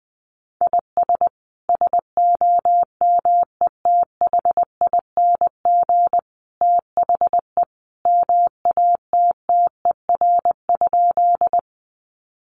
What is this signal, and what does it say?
Morse code, heard as a keyed tone, about 19 wpm, 700 Hz. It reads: IS SOMETHING THE MATTER?
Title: Sentences from Top 700 Words 20wpm